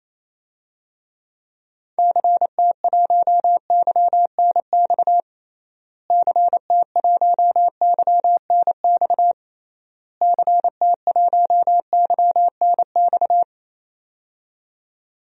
Morse code, heard as a keyed tone, about 28 wpm, 700 Hz. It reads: CT1YNX CT1YNX CT1YNX
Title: DXCC-100 Call Signs - 6 Characters - Repeated 3x - 28wpm